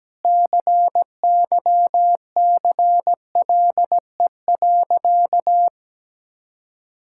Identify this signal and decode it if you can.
Morse code, heard as a keyed tone, about 17 wpm, 700 Hz. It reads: CYCLE.
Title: Sentences from Top 1000 Words - Review Part 6 - 17wpm